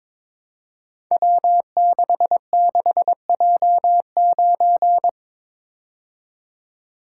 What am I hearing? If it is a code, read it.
Morse code, heard as a keyed tone, about 22 wpm, 700 Hz. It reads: W66J9
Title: Code Groups - 5 Letter:Numbers - 22wpm